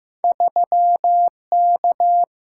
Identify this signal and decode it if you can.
Morse code, heard as a keyed tone, about 15 wpm, 700 Hz. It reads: 3K
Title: Single-Letter Single-Number 15wpm